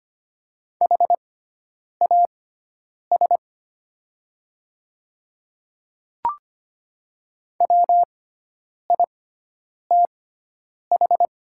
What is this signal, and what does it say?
Morse code, heard as a keyed tone, about 25 wpm, spaced as for 9 wpm, 700 Hz. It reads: HAS WITH
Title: 6x Character Spacing - Top 100 Words - 25wpm